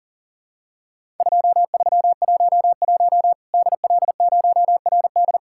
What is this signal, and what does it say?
Morse code, heard as a keyed tone, about 40 wpm, 700 Hz. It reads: 2311 DL0RD
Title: Worked All Europe DX Contest - QTC Exchange - 40wpm